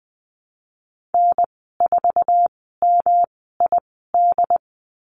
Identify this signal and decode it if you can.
Morse code, heard as a keyed tone, about 19 wpm, spaced as for 14 wpm, 700 Hz. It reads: N4MID
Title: 147 Lesson 24g - Letter y - Call Signs 20wpm